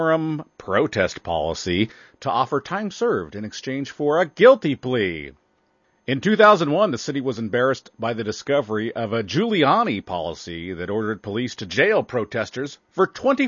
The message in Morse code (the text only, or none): none